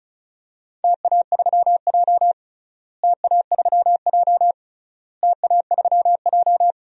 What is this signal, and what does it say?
Morse code, heard as a keyed tone, about 34 wpm, 700 Hz. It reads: TA3J TA3J TA3J
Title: DXCC-100 Call Signs - 4 Characters - Repeated 3x - 35wpm